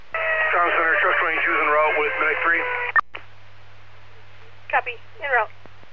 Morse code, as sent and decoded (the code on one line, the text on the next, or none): none